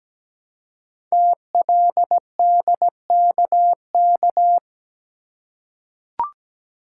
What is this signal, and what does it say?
Morse code, heard as a keyed tone, about 17 wpm, 700 Hz. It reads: TLDKK
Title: Code Groups - 5 Letters - 17wpm